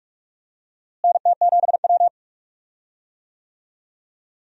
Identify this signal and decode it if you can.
Morse code, heard as a keyed tone, about 46 wpm, 700 Hz. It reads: NT7W
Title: USA Call Signs - 4 Characters - Repeated 2x - 45wpm